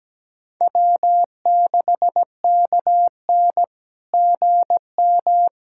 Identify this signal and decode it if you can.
Morse code, heard as a keyed tone, about 17 wpm, 700 Hz. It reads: W6KN GM
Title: SOTA QSO Practice 17wpm